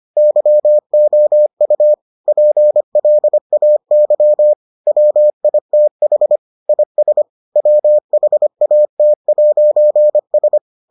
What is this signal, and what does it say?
Morse code, heard as a keyed tone, about 25 wpm, 600 Hz. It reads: YOU PLAY WITH IS WHAT'S